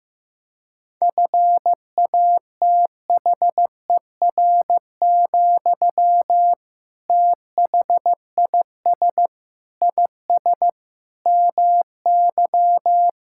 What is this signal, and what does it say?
Morse code, heard as a keyed tone, about 15 wpm, 700 Hz. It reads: FATHER, THIS IS MY
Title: Sentences from Top 1000 Words - Review Part 6 - 15wpm